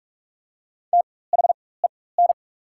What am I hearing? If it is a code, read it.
Morse code, heard as a keyed tone, about 46 wpm, spaced as for 21 wpm, 700 Hz. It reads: THEN
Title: 4x Character Spacing - Top 300 Words - 45wpm